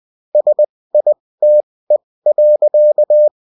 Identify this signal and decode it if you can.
Morse code, heard as a keyed tone, about 20 wpm, spaced as for 16 wpm, 600 Hz. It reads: SITE.
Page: bulletin-uft-sous-forme-texte-et-en-cw